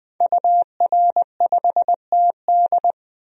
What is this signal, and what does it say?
Morse code, heard as a keyed tone, about 21 wpm, 700 Hz. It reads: UR5TD
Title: DXCC-100 Call Signs - 5 Characters - 20wpm